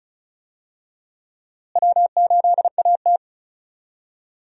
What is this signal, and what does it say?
Morse code, heard as a keyed tone, about 36 wpm, 700 Hz. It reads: W8AT
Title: USA Call Signs - 4 Characters - Repeated 2x - 35wpm